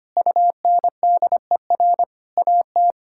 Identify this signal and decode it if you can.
Morse code, heard as a keyed tone, about 25 wpm, 700 Hz. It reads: UNDER AT
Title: Sentences from Top 1000 Words - Review Part 6 - 25wpm